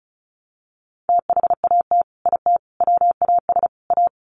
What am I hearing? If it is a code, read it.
Morse code, heard as a keyed tone, about 34 wpm, 700 Hz. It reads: THAT IT WAS A
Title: Sets of 4 Words 35wpm